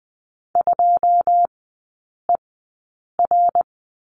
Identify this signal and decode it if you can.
Morse code, heard as a keyed tone, about 20 wpm, spaced as for 8 wpm, 700 Hz. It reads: 2ER
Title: 107 Lesson 18e - Letter m - Three Characters 20wpm